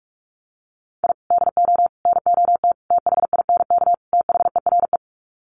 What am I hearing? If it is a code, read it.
Morse code, heard as a keyed tone, about 44 wpm, 700 Hz. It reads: I DO NOT THINK THERE
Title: Sets of 5 Words 45wpm